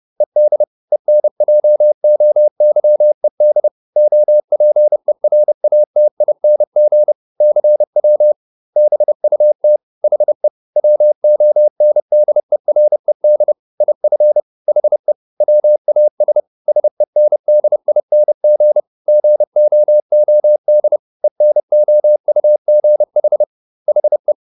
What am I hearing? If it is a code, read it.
Morse code, heard as a keyed tone, about 30 wpm, 600 Hz. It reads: ED ENJOYED OPERATING CW BUT HE WONDERED IF HE WAS SENDING GOOD ENOUGH HE